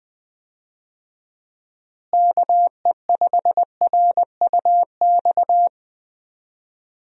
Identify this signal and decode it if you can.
Morse code, heard as a keyed tone, about 20 wpm, 700 Hz. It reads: KE5RUX
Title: USA Call Signs - 6 Characters - Repeated 2x - 20wpm